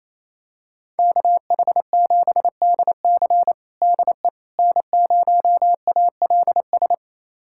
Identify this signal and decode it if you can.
Morse code, heard as a keyed tone, about 28 wpm, 700 Hz. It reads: KH7DC DE N0ALS
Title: USA Callsign DE Callsign - 5 Characters - 28wpm